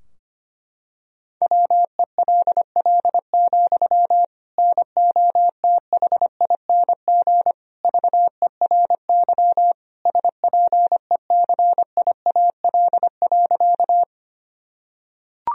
Morse code, heard as a keyed tone, about 25 wpm, 700 Hz. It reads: WELL, NOTHING VERY SPECIAL.
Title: Sentences from Top 1000 Words - Review Part 3 - 25wpm